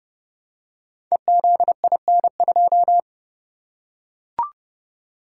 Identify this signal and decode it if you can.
Morse code, heard as a keyed tone, about 30 wpm, 700 Hz. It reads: EZIN2
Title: Code Groups - 5 Letter:Numbers - 30wpm